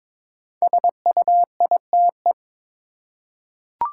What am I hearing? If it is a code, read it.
Morse code, heard as a keyed tone, about 22 wpm, 700 Hz. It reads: SUITE
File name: 5 Letter Words - First 3 Letters Given - 22wpm